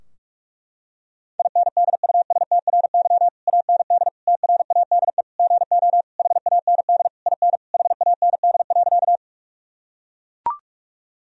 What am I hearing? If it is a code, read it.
Morse code, heard as a keyed tone, about 42 wpm, 700 Hz. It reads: INDUSTRY AND TRADE GO HAND IN HAND.
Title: Sentences from Top 1000 Words - Review Part 4 - 45wpm